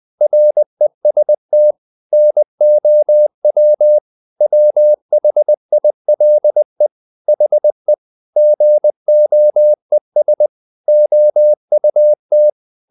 Morse code, heard as a keyed tone, about 20 wpm, 600 Hz. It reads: REST NOW WHILE HE GOES OUT
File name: the old man and the sea part 2 20wpm